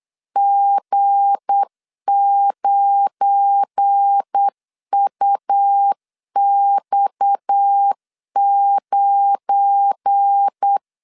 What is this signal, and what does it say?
Morse code, heard as a keyed tone, about 8 wpm, 800 Hz. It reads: G9UX9
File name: LEZIONI / 7 GF9Z / 7 GF9 8wpm 3545